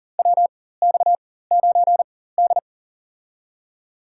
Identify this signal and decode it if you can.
Morse code, heard as a keyed tone, about 40 wpm, spaced as for 19 wpm, 700 Hz. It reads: WX9D
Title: USA Call Signs - 4 Characters - 4x Character Spacing - 40wpm